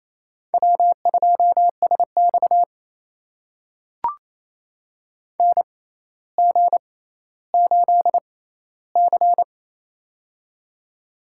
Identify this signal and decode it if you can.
Morse code, heard as a keyed tone, about 28 wpm, spaced as for 14 wpm, 700 Hz. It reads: W2SX NG8C
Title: USA Call Signs - 4 Characters - 6x Character Spacing - 28wpm